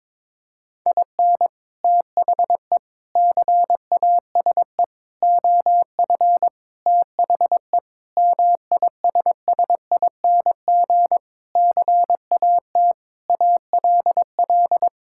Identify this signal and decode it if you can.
Morse code, heard as a keyed tone, about 22 wpm, 700 Hz. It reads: IN THE CASE OF THE MISSING CAT ALL
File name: practice / 22wpm / Sentences with 4-Word Phrases - 22wpm